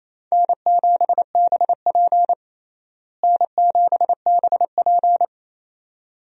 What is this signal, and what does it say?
Morse code, heard as a keyed tone, about 28 wpm, 700 Hz. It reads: N7BP N7BP
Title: USA Call Signs - 4 Characters - Repeated 2x - 28wpm